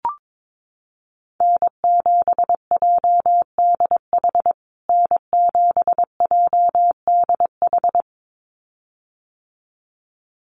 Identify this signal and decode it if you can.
Morse code, heard as a keyed tone, about 22 wpm, 700 Hz. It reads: N7JDH N7JDH
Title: USA Call Signs Repeated 2x 22wpm